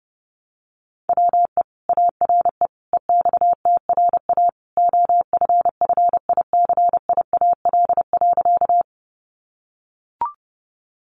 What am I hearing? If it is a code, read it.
Morse code, heard as a keyed tone, about 30 wpm, 700 Hz. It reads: WE ARE EXTRA OFFICIAL.
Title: Sentences from Top 1000 Words 30wpm